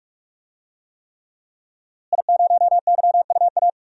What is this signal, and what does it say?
Morse code, heard as a keyed tone, about 42 wpm, 700 Hz. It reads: I0YUA
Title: DXCC-100 Call Signs - 5 Characters - Repeated 3x - 45wpm